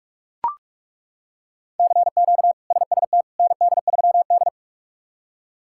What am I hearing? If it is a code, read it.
Morse code, heard as a keyed tone, about 42 wpm, 700 Hz. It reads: CQ SST ND3D
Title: K1USN SST QSO Practice - 45wpm